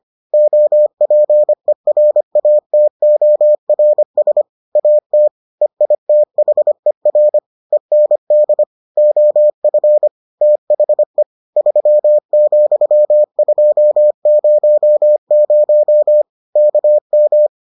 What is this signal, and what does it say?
Morse code, heard as a keyed tone, about 25 wpm, 600 Hz. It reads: OPERATORS AT EITHER END OF THE 3,200 KM